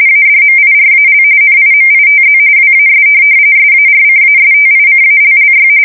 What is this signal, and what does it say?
RTTY, about 45.45 baud. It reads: ADMIRAL NIMITZ WAS SARCASTICALLY
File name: fsk170 / USS-New-Jersey